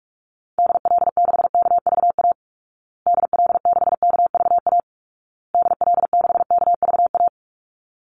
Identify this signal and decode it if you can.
Morse code, heard as a keyed tone, about 44 wpm, 700 Hz. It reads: DL6KVA DL6KVA DL6KVA
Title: Call Signs Repeated 2x - 45wpm